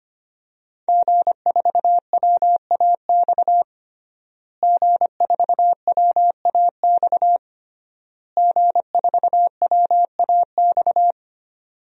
Morse code, heard as a keyed tone, about 25 wpm, 700 Hz. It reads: G4WAX G4WAX G4WAX
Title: DXCC-100 Call Signs - 5 Characters - Repeated 3x - 25wpm